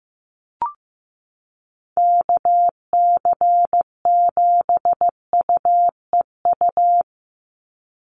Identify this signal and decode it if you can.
Morse code, heard as a keyed tone, about 15 wpm, 700 Hz. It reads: KC7UEU
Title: USA Call Signs 15wpm